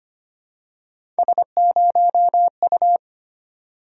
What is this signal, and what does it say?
Morse code, heard as a keyed tone, about 25 wpm, 700 Hz. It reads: S0U
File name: Three Letter-Number Combinations - 25wpm